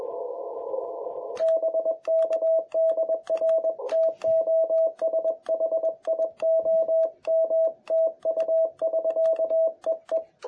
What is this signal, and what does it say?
Morse code, heard as a keyed tone, about 21 wpm, 650 Hz. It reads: BK DR TOSHI OM TU <SK> EE
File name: jh8cbh / 2021cw